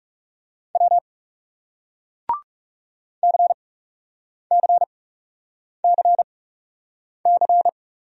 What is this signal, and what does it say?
Morse code, heard as a keyed tone, about 40 wpm, spaced as for 8 wpm, 700 Hz. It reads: W CCCC